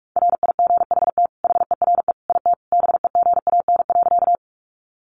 Morse code, heard as a keyed tone, about 44 wpm, 700 Hz. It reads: RIGHT HERE IT BEGAN.
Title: Sentences from Top 300 Words 45wpm